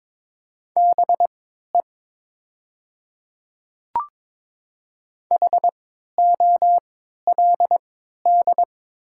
Morse code, heard as a keyed tone, about 22 wpm, spaced as for 12 wpm, 690 Hz. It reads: BE HOLD
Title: 3x Character Spacing - Top 500 Words - 22wpm